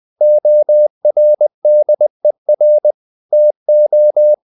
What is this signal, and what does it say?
Morse code, heard as a keyed tone, about 20 wpm, 600 Hz. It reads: ORDER TO
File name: the old man and the sea / the old man and the sea part 1 20wpm